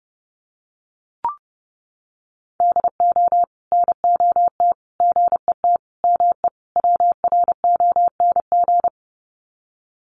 Morse code, heard as a keyed tone, about 30 wpm, 700 Hz. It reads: DO NOT GET ME WRONG
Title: Sets of 5 Words Encore 30wpm